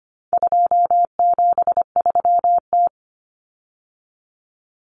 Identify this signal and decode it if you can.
Morse code, heard as a keyed tone, about 25 wpm, 700 Hz. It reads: 273T
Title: Common Cut Numbers - 4-Digit 25wpm